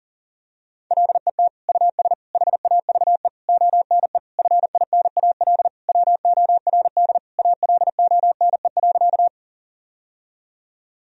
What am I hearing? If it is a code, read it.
Morse code, heard as a keyed tone, about 40 wpm, 700 Hz. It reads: LET US HAVE ONE FINAL WORD ALONE.
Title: Sentences from Top 1000 Words - Review Part 2 - 40wpm